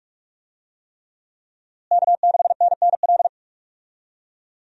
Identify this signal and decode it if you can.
Morse code, heard as a keyed tone, about 42 wpm, 700 Hz. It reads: K6NNL
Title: USA Call Signs - 5 Characters - Repeated 3x - 45wpm